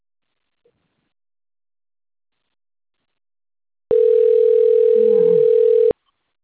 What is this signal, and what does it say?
Ringback tone (North America), 440 + 480 Hz.